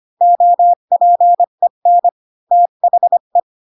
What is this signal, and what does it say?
Morse code, heard as a keyed tone, about 25 wpm, 700 Hz. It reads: OPEN THE
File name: Pinocchio / Pinocchio ch7 20 25 ik1hge